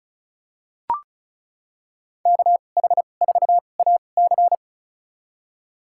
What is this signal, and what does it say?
Morse code, heard as a keyed tone, about 36 wpm, spaced as for 25 wpm, 700 Hz. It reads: KH4AC
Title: USA Call Signs - 5 Characters - 2x Character Spacing - 35wpm